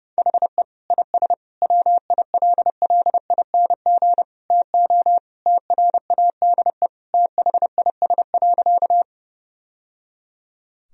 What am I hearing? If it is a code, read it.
Morse code, heard as a keyed tone, about 30 wpm, 700 Hz. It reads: HE IS WILLING TO TRADE THIS.
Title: Sentences from Top 1000 Words - Review Part 4 - 30wpm